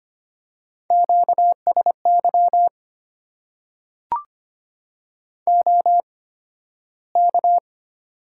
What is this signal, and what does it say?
Morse code, heard as a keyed tone, about 25 wpm, spaced as for 13 wpm, 690 Hz. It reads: QSY OK
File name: Top 90 QSO Elements - 8x Character Spacing - 25wpm